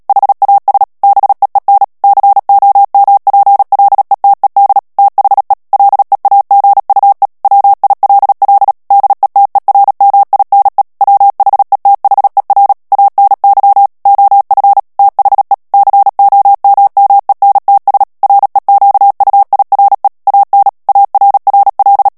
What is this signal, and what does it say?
Morse code, heard as a keyed tone, about 37 wpm, 800 Hz. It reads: HAS BEEN COMPLETED THE LEAGUE WILL DETERMINE WHETHER ANY OF THE COMMENTS REQUIRE AN ARRL
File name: morse mp3 / ANGLAIS texte clair 3 vitesse 35 mots minute